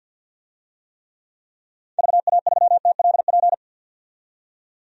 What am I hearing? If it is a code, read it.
Morse code, heard as a keyed tone, about 50 wpm, 700 Hz. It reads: VA3TLP
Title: DXCC-100 Call Signs - 6 Characters - 50wpm